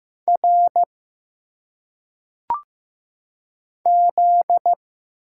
Morse code, heard as a keyed tone, about 15 wpm, 700 Hz. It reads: R Z
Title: Single Letters - Phonetic Pronunciation 15wpm